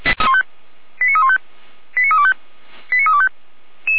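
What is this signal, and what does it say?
Five-tone selective call (ZVEI), heard as quick runs of five tones, tones of about 70 ms. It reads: 89315 89325 89325
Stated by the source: ZVEI five-tone call 89320 received by SDR.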